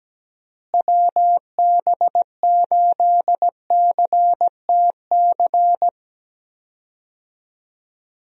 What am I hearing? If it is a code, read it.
Morse code, heard as a keyed tone, about 17 wpm, 700 Hz. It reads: WB8CTC
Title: USA Call Signs - 6 Characters - Repeated 3x - 17wpm